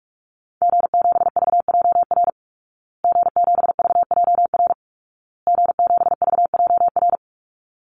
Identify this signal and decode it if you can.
Morse code, heard as a keyed tone, about 44 wpm, 700 Hz. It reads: G7VJR G7VJR G7VJR
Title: Call Signs Repeated 2x - 45wpm